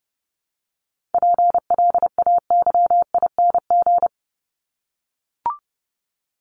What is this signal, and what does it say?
Morse code, heard as a keyed tone, about 30 wpm, 700 Hz. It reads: PLAYING
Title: ING Suffix - Top 100 Words with Base Words 30wpm